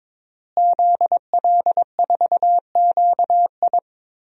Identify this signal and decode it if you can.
Morse code, heard as a keyed tone, about 21 wpm, 700 Hz. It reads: ZL4QI
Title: DXCC-100 Call Signs - 5 Characters - Repeated 3x - 22wpm